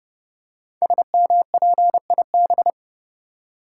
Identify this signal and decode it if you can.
Morse code, heard as a keyed tone, about 29 wpm, 700 Hz. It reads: SMPIB
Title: Code Groups - 5 Letters - 30wpm